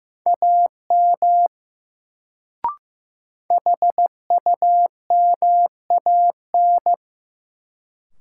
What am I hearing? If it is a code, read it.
Morse code, heard as a keyed tone, about 15 wpm, 700 Hz. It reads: AM HUMAN
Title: Sentences from Top 600 Words - New Word Drill - 15wpm